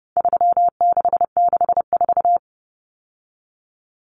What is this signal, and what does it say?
Morse code, heard as a keyed tone, about 30 wpm, 700 Hz. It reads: 3664
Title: Common Cut Numbers - 4-Digit 30wpm